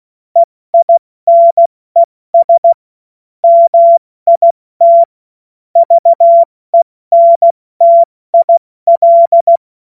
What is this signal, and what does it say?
Morse code, heard as a keyed tone, about 16 wpm, spaced as for 14 wpm, 680 Hz. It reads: EINES MIT VENTIL
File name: Morsekurs / Witze / Witz 27 Mai 2025 70 F80